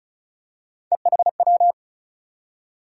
Morse code, heard as a keyed tone, about 36 wpm, 700 Hz. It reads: EHW
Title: USA Call Sign Postfixes - 3 Letters 35wpm